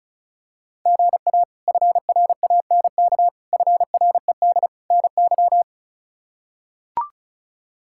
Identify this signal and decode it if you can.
Morse code, heard as a keyed tone, about 36 wpm, 700 Hz. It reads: GA FRANK FRED NY
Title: K1USN SST QSO Practice - 35wpm